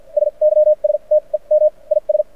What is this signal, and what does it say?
Morse code, heard as a keyed tone, about 53 wpm, spaced as for 38 wpm, 610 Hz. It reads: SYSTEM IS
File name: archived / CRSnet